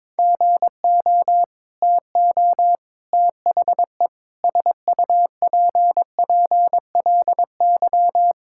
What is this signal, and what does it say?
Morse code, heard as a keyed tone, about 22 wpm, 700 Hz. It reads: GO TO THE SUPPLY
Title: Sentences from Top 1000 Words - Review Part 5 - 22wpm